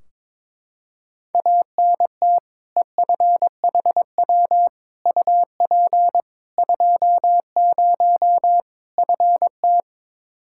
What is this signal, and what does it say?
Morse code, heard as a keyed tone, about 22 wpm, 700 Hz. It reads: ANT EFHW UP 20 FT
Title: Practice QSO - Exchange 2 - Info - 22wpm